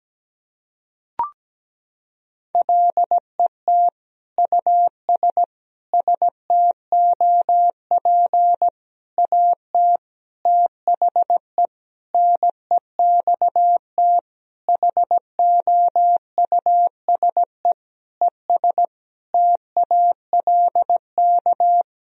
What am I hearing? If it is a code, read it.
Morse code, heard as a keyed tone, about 17 wpm, 700 Hz. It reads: LET US STOP AT THE NEXT HOUSE ES TALK
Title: Sentences from Top 400 Words and Abbreviations 17wpm